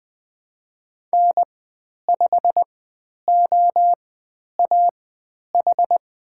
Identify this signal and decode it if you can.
Morse code, heard as a keyed tone, about 20 wpm, spaced as for 10 wpm, 700 Hz. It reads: N5OAH
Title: 109 Lesson 18g - Letter m - Call Signs 20wpm